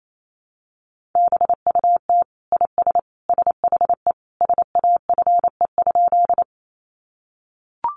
Morse code, heard as a keyed tone, about 28 wpm, 700 Hz. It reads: BUT IS SHE SAFE?
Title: Sentences from Top 900 Words 28wpm